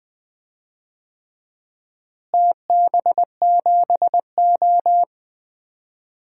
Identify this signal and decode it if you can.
Morse code, heard as a keyed tone, about 20 wpm, 700 Hz. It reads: TB7O
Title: DXCC-100 Call Signs - 4 Characters - Repeated 3x - 20wpm